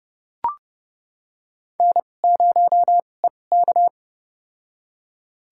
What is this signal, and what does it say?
Morse code, heard as a keyed tone, about 30 wpm, spaced as for 22 wpm, 700 Hz. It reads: N0EK
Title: USA Call Signs - 2x Character Spacing - 30wpm